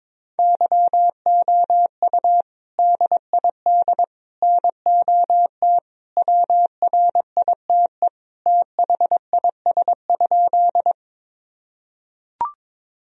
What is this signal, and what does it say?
Morse code, heard as a keyed tone, about 22 wpm, 700 Hz. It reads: YOU DID NOT WRITE THIS?
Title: Sentences from Top 1000 Words - Review Part 1 - 22wpm